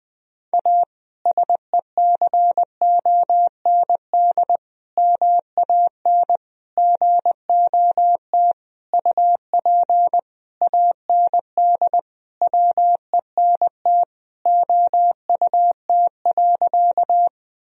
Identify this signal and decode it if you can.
Morse code, heard as a keyed tone, about 20 wpm, 700 Hz. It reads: A SECOND MAN GOT UP AND WENT OUT.
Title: Sentences from Top 1000 Words - Review Part 1 - 20wpm